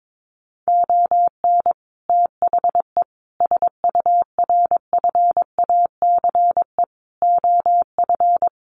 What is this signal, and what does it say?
Morse code, heard as a keyed tone, about 22 wpm, 700 Hz. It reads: ON THE SURFACE OF
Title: Sets of 5 Words Encore 22wpm